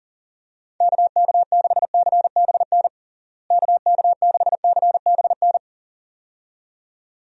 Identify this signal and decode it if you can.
Morse code, heard as a keyed tone, about 40 wpm, 700 Hz. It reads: KK6CBN KK6CBN
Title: USA Call Signs - 6 Characters - Repeated 2x - 40wpm